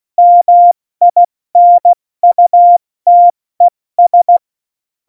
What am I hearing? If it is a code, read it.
Morse code, heard as a keyed tone, about 16 wpm, 700 Hz. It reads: MINUTES
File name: Morsekurs / Witze / Witz 25 Juli 17, Tempo 70 F80